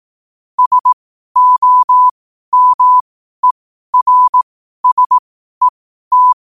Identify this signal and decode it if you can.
Morse code, heard as a keyed tone, about 18 wpm, spaced as for 12 wpm, 1000 Hz. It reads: SOMERSET
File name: qotdfiles / 12wpm